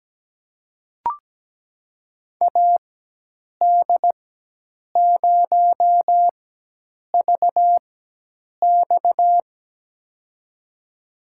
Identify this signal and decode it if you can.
Morse code, heard as a keyed tone, about 17 wpm, spaced as for 8 wpm, 700 Hz. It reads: AD0VX
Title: USA Call Signs - 5 Characters - 4x Character Spacing - 17wpm